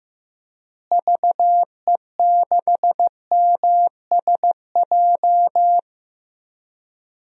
Morse code, heard as a keyed tone, about 15 wpm, 700 Hz. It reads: VE6MSJ
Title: DXCC-100 Call Signs - 6 Characters - Repeated 3x - 15wpm